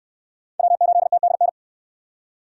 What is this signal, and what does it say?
Morse code, heard as a keyed tone, about 67 wpm, 700 Hz. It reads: W8TDN